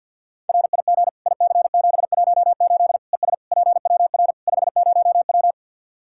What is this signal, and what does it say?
Morse code, heard as a keyed tone, about 47 wpm, 700 Hz. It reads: RIG IC718 ES PWR 50W
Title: Practice QSO - Exchange 2 - Info - 50wpm